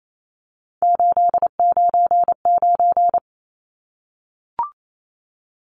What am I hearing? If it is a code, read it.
Morse code, heard as a keyed tone, about 28 wpm, 700 Hz. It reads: 899
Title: Numbers - 3-Digit 28wpm